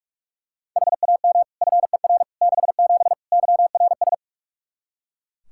Morse code, heard as a keyed tone, about 42 wpm, 700 Hz. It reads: HAM FER 67 YRS
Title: Practice QSO - Exchange 3 - More Info - 45wpm